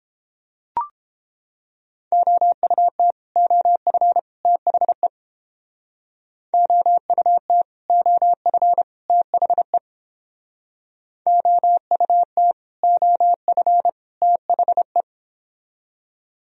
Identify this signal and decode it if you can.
Morse code, heard as a keyed tone, about 29 wpm, 700 Hz. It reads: OUT OF THE OUT OF THE OUT OF THE